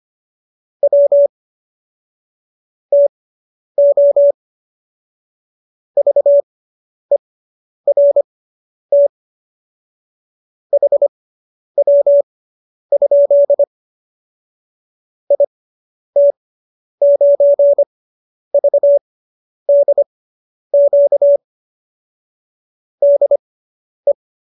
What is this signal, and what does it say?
Morse code, heard as a keyed tone, about 25 wpm, spaced as for 10 wpm, 570 Hz. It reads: W TO VERT HW? IT9VDQ DE